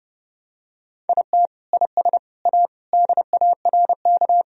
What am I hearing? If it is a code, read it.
Morse code, heard as a keyed tone, about 30 wpm, 700 Hz. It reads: IT IS A DARK
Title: Sentences from Top 1000 Words - Review Part 1 - 30wpm